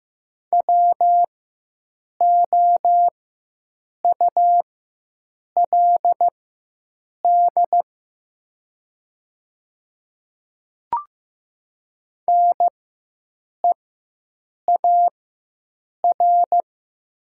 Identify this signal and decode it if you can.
Morse code, heard as a keyed tone, about 15 wpm, spaced as for 7 wpm, 700 Hz. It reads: WOULD NEAR
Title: 4x Character Spacing - Top 200 Words - 15wpm